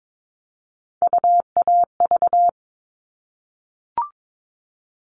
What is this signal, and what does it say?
Morse code, heard as a keyed tone, about 22 wpm, 700 Hz. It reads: UAV
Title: All Cut Numbers - 3-Digit 22wpm